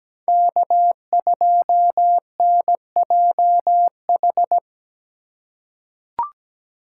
Morse code, heard as a keyed tone, about 17 wpm, 700 Hz. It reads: K2NJH
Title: USA Call Signs - 2x Character Spacing - 17wpm